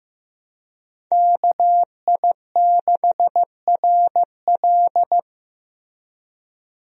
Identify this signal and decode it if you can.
Morse code, heard as a keyed tone, about 15 wpm, 700 Hz. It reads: KI6RL
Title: USA Call Signs - 5 Characters - Repeated 2x - 15wpm